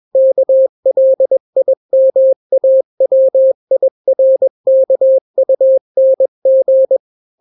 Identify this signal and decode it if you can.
Morse code, heard as a keyed tone, about 21 wpm, 530 Hz. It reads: KLIMAWIRKUNG
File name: daily / 20wpm